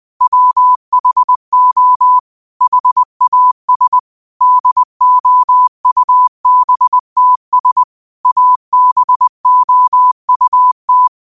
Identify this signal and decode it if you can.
Morse code, heard as a keyed tone, about 20 wpm, 1000 Hz. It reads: WHO HAS DOUBTS ABOUT
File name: qotdfiles / 20wpm